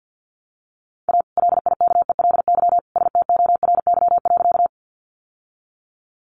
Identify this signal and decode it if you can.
Morse code, heard as a keyed tone, about 50 wpm, 700 Hz. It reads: A LIKELY STORY.
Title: Sentences from Top 600 Words 50wpm